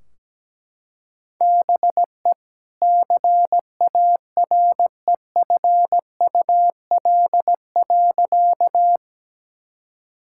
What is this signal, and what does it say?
Morse code, heard as a keyed tone, about 17 wpm, 700 Hz. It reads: BE CAREFUL.
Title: Sentences from Top 1600 Words 17wpm